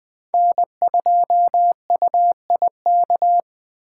Morse code, heard as a keyed tone, about 20 wpm, 700 Hz. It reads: N2UIK